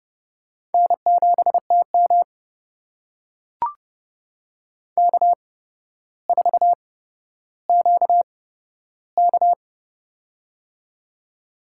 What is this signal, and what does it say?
Morse code, heard as a keyed tone, about 30 wpm, spaced as for 13 wpm, 700 Hz. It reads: N7TM K4QK
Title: USA Call Signs - 4 Characters - 8x Character Spacing - 30wpm